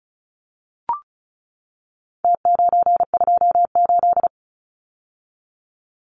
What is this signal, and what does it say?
Morse code, heard as a keyed tone, about 34 wpm, 700 Hz. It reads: T928
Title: Numbers - Contest Style - 4-Digit 35wpm